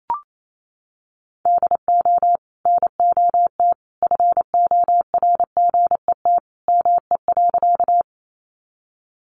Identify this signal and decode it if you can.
Morse code, heard as a keyed tone, about 28 wpm, 700 Hz. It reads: DO NOT FORGET ME.
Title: Sentences from Top 1200 Words 28wpm